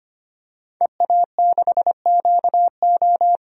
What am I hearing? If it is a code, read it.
Morse code, heard as a keyed tone, about 25 wpm, 700 Hz. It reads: EA6QO
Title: DXCC-100 Call Signs - 5 Characters - 25wpm